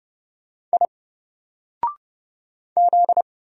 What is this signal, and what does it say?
Morse code, heard as a keyed tone, about 30 wpm, 700 Hz. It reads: I Z
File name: Single-Letter-Number-30wpm